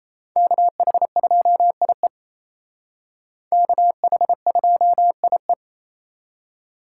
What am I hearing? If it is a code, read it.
Morse code, heard as a keyed tone, about 35 wpm, 700 Hz. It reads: KH2IE KH2IE
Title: Speed Racing - USA 2x2 Call Signs 28wpm